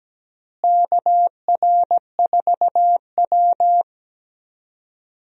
Morse code, heard as a keyed tone, about 17 wpm, 700 Hz. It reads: KR4W
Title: USA Call Signs - 4 Characters - Repeated 3x - 17wpm